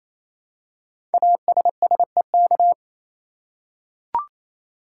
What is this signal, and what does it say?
Morse code, heard as a keyed tone, about 28 wpm, 700 Hz. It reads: ASSEK
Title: Code Groups - 5 Letters - 28wpm